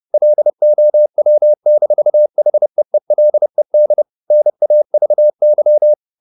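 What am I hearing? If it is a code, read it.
Morse code, heard as a keyed tone, about 30 wpm, 600 Hz. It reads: LOW-HEELED NAVY
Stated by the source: Morse code at 30 wpm, the novel Never by Ken Follett in English, from CW med Gnister.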